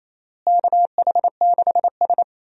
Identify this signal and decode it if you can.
Morse code, heard as a keyed tone, about 28 wpm, 700 Hz. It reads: KH6S
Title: DXCC-100 Call Signs - 4 Characters - Repeated 3x - 28wpm